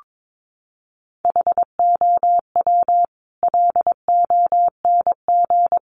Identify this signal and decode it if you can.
Morse code, heard as a keyed tone, about 22 wpm, 700 Hz. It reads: HOW LONG
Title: Sentences from Top 500 Words 22wpm